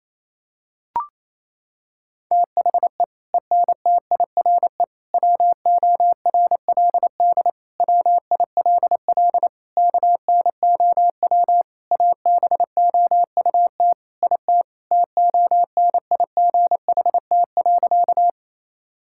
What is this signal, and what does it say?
Morse code, heard as a keyed tone, about 28 wpm, 700 Hz. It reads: THE ENTIRE WORLD WILL KNOW ABOUT IT TONIGHT.
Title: Sentences from Top 1000 Words - Review Part 5 - 28wpm